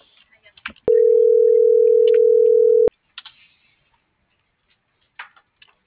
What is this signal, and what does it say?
Ringback tone (North America), 440 + 480 Hz.
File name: CENTRAL SOTOVA / out alejandro.rodriguez - .56549